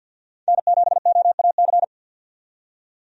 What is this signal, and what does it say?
Morse code, heard as a keyed tone, about 50 wpm, 700 Hz. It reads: N7OAC
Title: USA Call Signs - 5 Characters - 50wpm